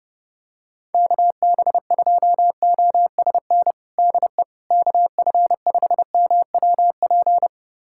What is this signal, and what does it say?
Morse code, heard as a keyed tone, about 30 wpm, 700 Hz. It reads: KB2OSN DE KF5MWP
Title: USA Callsign DE Callsign 30wpm